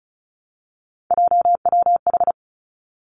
Morse code, heard as a keyed tone, about 36 wpm, 700 Hz. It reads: JWH
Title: Three Challenging Letter-Number Combinations 35wpm